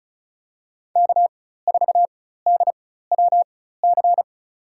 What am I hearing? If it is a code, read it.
Morse code, heard as a keyed tone, about 36 wpm, spaced as for 16 wpm, 700 Hz. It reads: K4DWC
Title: USA Call Signs - 4x Character Spacing - 35wpm